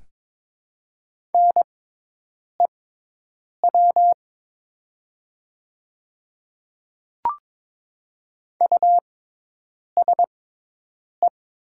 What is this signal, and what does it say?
Morse code, heard as a keyed tone, about 22 wpm, spaced as for 8 wpm, 700 Hz. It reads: NEW USE
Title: 6x Character Spacing - Top 300 Words - 22wpm